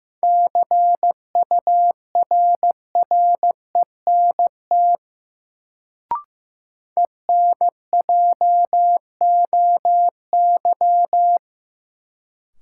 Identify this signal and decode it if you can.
Morse code, heard as a keyed tone, about 15 wpm, 700 Hz. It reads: CURRENT ENJOY
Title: Sentences from Top 1000 Words - Review Part 4 - New Word Drill - 15wpm